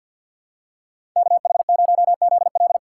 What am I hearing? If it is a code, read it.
Morse code, heard as a keyed tone, about 47 wpm, 700 Hz. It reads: KH0ZL
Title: USA Call Signs - 5 Characters - 6x Character Spacing - 50wpm